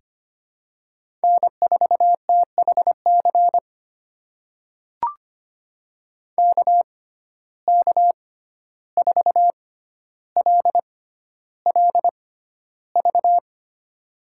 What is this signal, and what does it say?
Morse code, heard as a keyed tone, about 25 wpm, spaced as for 12 wpm, 700 Hz. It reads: N4THC KK4LLV
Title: USA Call Signs - 6x Character Spacing - 25wpm